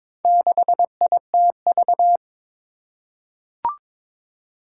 Morse code, heard as a keyed tone, about 22 wpm, 700 Hz. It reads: 6ITV
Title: Four Challenging Letter-Number Combinations 22wpm